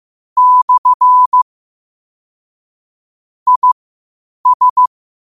Morse code, heard as a keyed tone, about 15 wpm, spaced as for 8 wpm, 1000 Hz. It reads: / IS
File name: qotdfiles / 8wpm